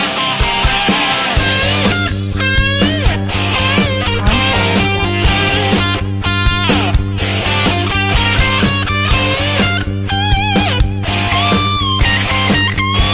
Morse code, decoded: none